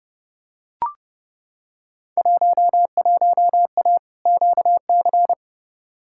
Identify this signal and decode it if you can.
Morse code, heard as a keyed tone, about 30 wpm, 700 Hz. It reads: 11A QC
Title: Field Day Exchanges Encore - 30wpm